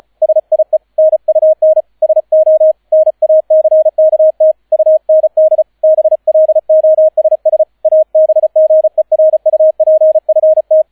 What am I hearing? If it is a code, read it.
Morse code, heard as a keyed tone, about 36 wpm, 630 Hz, with radio noise. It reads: SIE NUN SO NACKT UND BLOSS ABGERUPFT